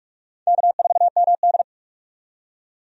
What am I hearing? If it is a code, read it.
Morse code, heard as a keyed tone, about 46 wpm, 700 Hz. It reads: K4MD
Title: USA Call Signs - 4 Characters - 45wpm